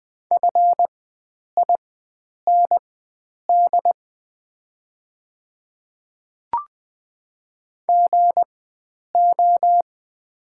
Morse code, heard as a keyed tone, about 20 wpm, spaced as for 9 wpm, 700 Hz. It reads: FIND GO
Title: 4x Character Spacing - Top 100 Words - 20wpm